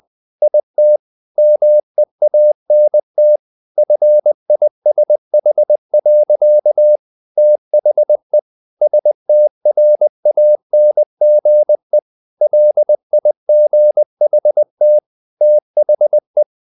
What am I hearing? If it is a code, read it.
Morse code, heard as a keyed tone, about 20 wpm, 600 Hz. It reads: IT MEANT FISH. THE STRANGE LIGHT THE
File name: the old man and the sea / the old man and the sea part 1 20wpm